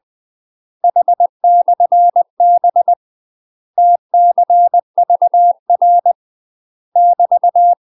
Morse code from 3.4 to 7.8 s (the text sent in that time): TCVR <BT>